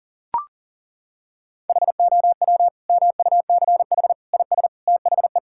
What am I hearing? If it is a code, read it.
Morse code, heard as a keyed tone, about 37 wpm, 700 Hz. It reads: HOW MUCH IS THE
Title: Sentences from Top 1000 Words - Review Part 6 - 40wpm